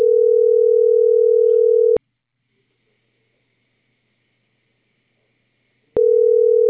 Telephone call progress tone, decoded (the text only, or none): ringback tone